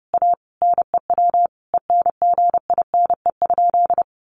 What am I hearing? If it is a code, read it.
Morse code, heard as a keyed tone, about 30 wpm, 700 Hz. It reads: A NEW ENGINE?
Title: Sentences from Top 1500 Words 30wpm